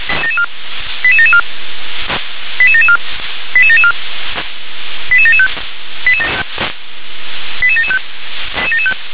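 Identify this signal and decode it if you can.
Five-tone selective call (ZVEI), heard as quick runs of five tones, tones of about 70 ms. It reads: 88774 88774 88774 88775 88775
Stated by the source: ZVEI five-tone call received on SDR.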